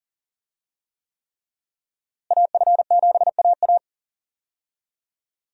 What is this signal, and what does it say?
Morse code, heard as a keyed tone, about 40 wpm, 700 Hz. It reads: AF7AA
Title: USA Call Signs - 5 Characters - Repeated 3x - 40wpm